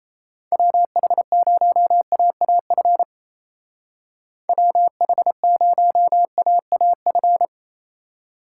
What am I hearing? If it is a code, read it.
Morse code, heard as a keyed tone, about 35 wpm, 700 Hz. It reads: WH0AAF WH0AAF